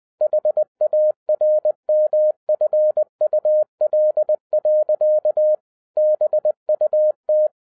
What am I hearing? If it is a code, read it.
Morse code, heard as a keyed tone, about 20 wpm, 610 Hz. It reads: HARMFUL. BUT